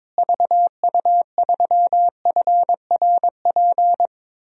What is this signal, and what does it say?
Morse code, heard as a keyed tone, about 21 wpm, 700 Hz. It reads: VU3FRP
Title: DXCC-100 Call Signs - 6 Characters - Repeated 3x - 22wpm